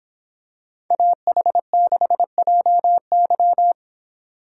Morse code, heard as a keyed tone, about 24 wpm, 700 Hz. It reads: AH6JY